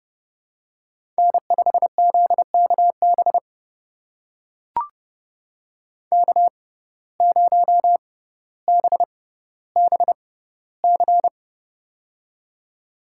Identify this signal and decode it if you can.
Morse code, heard as a keyed tone, about 32 wpm, spaced as for 16 wpm, 700 Hz. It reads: N5ZKB K0BBC
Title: USA Call Signs - 5 Characters - 6x Character Spacing - 30wpm